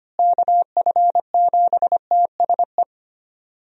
Morse code, heard as a keyed tone, about 25 wpm, 700 Hz. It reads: KF7TSE
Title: USA Call Signs - 6 Characters - Repeated 3x - 25wpm